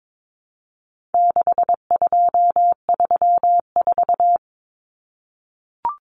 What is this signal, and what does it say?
Morse code, heard as a keyed tone, about 22 wpm, 700 Hz. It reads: 6234
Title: Common Cut Numbers - 4-Digit 22wpm